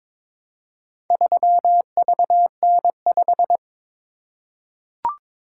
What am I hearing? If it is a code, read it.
Morse code, heard as a keyed tone, about 23 wpm, 700 Hz. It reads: 3VN5
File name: practice / 22wpm / Four Letter-Number Combinations - 22wpm